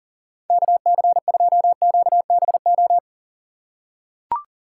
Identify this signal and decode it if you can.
Morse code, heard as a keyed tone, about 40 wpm, 690 Hz. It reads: KC2QBO
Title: USA Call Signs - 3x Character Spacing - 40wpm